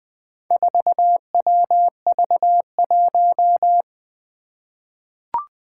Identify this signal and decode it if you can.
Morse code, heard as a keyed tone, about 20 wpm, 700 Hz. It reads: 4WV1
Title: Four Challenging Letter-Number Combinations 20wpm